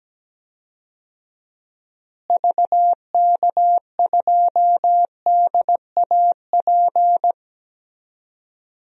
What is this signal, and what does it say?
Morse code, heard as a keyed tone, about 17 wpm, 700 Hz. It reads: VK2DAP
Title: DXCC-100 Call Signs - 6 Characters - Repeated 3x - 17wpm